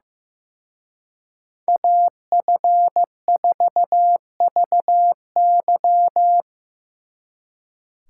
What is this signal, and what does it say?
Morse code, heard as a keyed tone, about 17 wpm, 700 Hz. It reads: AF4VY
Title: Speed Racing - USA 2x2 Call Signs 15wpm